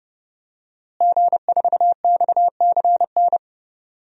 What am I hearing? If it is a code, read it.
Morse code, heard as a keyed tone, about 30 wpm, 700 Hz. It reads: G4XCN
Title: DXCC-100 Call Signs - 5 Characters - Repeated 2x - 30wpm